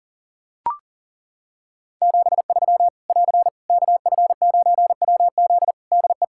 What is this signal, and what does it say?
Morse code, heard as a keyed tone, about 40 wpm, 700 Hz. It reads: 73 <AR> KF9WZ DE
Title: Practice QSO - Exchange 4 - Ending - 40wpm